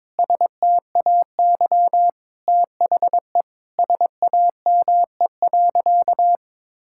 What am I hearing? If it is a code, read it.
Morse code, heard as a keyed tone, about 22 wpm, 700 Hz. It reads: STAY THE SAME.
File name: practice / 22wpm / Sentences with 8-Word Phrases - 22wpm